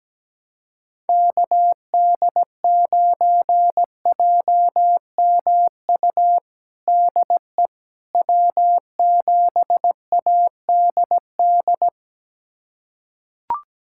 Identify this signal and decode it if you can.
Morse code, heard as a keyed tone, about 17 wpm, 700 Hz. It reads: KD9JMU DE W7ADD
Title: USA Callsign DE Callsign 17wpm